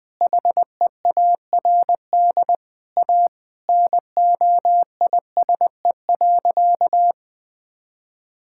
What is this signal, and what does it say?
Morse code, heard as a keyed tone, about 20 wpm, 700 Hz. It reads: HEARD A NOISE.
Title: Sentences with 5-Word Phrases - Spoken Beginning - 20wpm